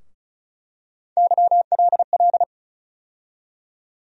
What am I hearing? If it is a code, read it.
Morse code, heard as a keyed tone, about 36 wpm, 700 Hz. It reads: YLL
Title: USA Call Sign Postfixes - 3 Letters 35wpm